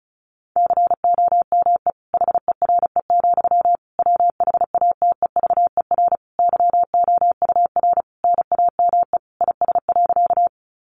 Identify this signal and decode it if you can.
Morse code, heard as a keyed tone, about 34 wpm, 700 Hz. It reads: COME HERE, WHATEVER YOUR NAME IS.
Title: Sentences from Top 1100 Words 35wpm